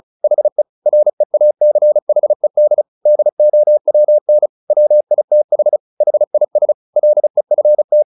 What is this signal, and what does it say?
Morse code, heard as a keyed tone, about 34 wpm, 600 Hz. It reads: HE REACHED DOWN WITH HIS LEFT
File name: the old man and the sea / the old man and the sea 1 35wpm